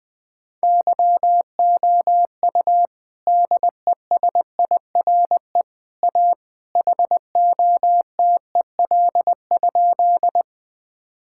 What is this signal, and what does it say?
Morse code, heard as a keyed tone, about 20 wpm, 700 Hz. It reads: YOU DESIRE A HOTEL?
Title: Sentences from Top 1600 Words 20wpm